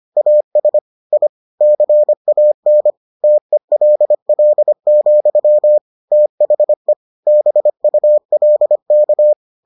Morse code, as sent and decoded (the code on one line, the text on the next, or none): .- ... / .. / -.-. .- -. / - . .-.. .-.. --..-- / - .... / -... ..- .-.. -.-
AS I CAN TELL, THE BULK